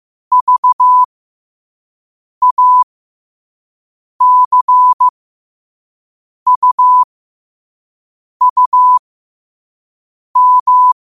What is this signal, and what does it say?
Morse code, heard as a keyed tone, about 15 wpm, spaced as for 5 wpm, 1000 Hz. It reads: VACUUM
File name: qotdfiles / 5wpm